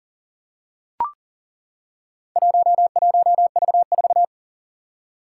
Morse code, heard as a keyed tone, about 40 wpm, 700 Hz. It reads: 11V4
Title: Four Challenging Letter-Number Combinations 40wpm